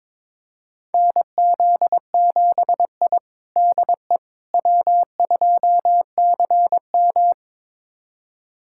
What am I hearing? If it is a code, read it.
Morse code, heard as a keyed tone, about 22 wpm, 700 Hz. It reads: NZ7I DE W2CM
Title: USA Callsign DE Callsign - 4 Characters - 22wpm